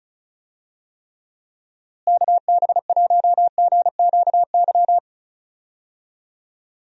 Morse code, heard as a keyed tone, about 36 wpm, 700 Hz. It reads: KB1GQY
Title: USA Call Signs - 6 Characters - Repeated 3x - 35wpm